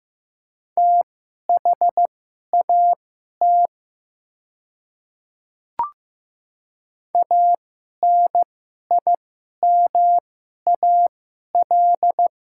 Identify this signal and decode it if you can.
Morse code, heard as a keyed tone, about 15 wpm, spaced as for 11 wpm, 700 Hz. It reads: THAT ANIMAL
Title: 2x Character Spacing - Top 500 Words - 15wpm